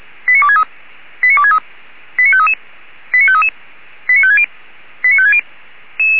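Five-tone selective call (ZVEI), heard as quick runs of five tones, tones of about 70 ms. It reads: 89263 89263 89533 89533 89560 89560
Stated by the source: ZVEI five-tone call 89456 received by SDR.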